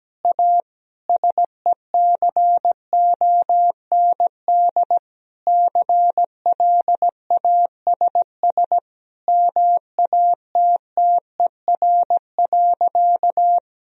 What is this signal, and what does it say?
Morse code, heard as a keyed tone, about 17 wpm, 700 Hz. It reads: A SECOND CLASS MATTER.
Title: Sentences from Top 1000 Words - Review Part 2 - 17wpm